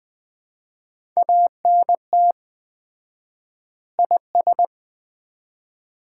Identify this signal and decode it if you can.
Morse code, heard as a keyed tone, about 20 wpm, 700 Hz. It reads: ANT IS
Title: Common QSO Phrases - 4x Word-Spacing - 20wpm